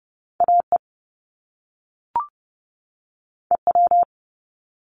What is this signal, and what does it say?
Morse code, heard as a keyed tone, about 30 wpm, 700 Hz. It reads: AE EW
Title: Top Two Letter Combinations in Words 30wpm